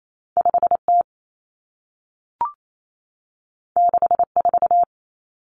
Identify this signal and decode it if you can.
Morse code, heard as a keyed tone, about 28 wpm, 700 Hz. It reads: 5T 64
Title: Common Cut Numbers - 2-Digit 28wpm